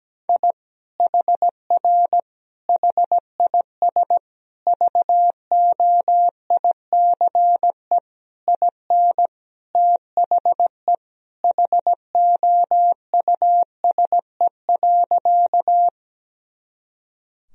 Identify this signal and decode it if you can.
Morse code, heard as a keyed tone, about 17 wpm, 700 Hz. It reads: I HR HIS VOICE IN THE HOUSE.
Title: Sentences from Top 500 Words and Abbreviations 17wpm